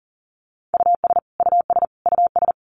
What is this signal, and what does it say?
Morse code, heard as a keyed tone, about 40 wpm, 700 Hz. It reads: US US US
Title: Top 500 Words Repeated 40wpm